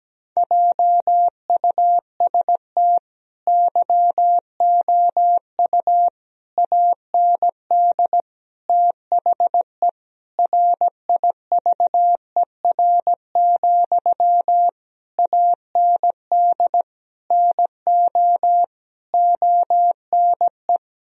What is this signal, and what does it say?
Morse code, heard as a keyed tone, about 17 wpm, 700 Hz. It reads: JUST YOU AND THE RIVER, AND NO ONE
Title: Sentences from Top 1000 Words - Review Part 1 - 17wpm